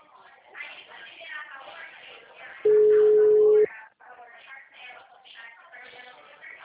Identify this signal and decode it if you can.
Ringback tone (Europe, 425 Hz), with speech.
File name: out- -2009- - - .210